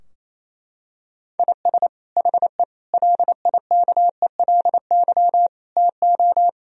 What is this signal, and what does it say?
Morse code, heard as a keyed tone, about 28 wpm, 700 Hz. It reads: IS HE LIKELY TO
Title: Sentences from Top 1000 Words - Review Part 4 - 28wpm